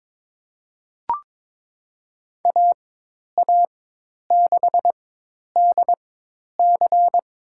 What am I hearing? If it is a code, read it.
Morse code, heard as a keyed tone, about 21 wpm, spaced as for 10 wpm, 700 Hz. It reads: AA6DC
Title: USA Call Signs - 5 Characters - 4x Character Spacing - 22wpm